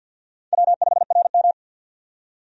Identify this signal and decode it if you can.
Morse code, heard as a keyed tone, about 52 wpm, 700 Hz. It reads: W5RM
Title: USA Call Signs - 4 Characters - 50wpm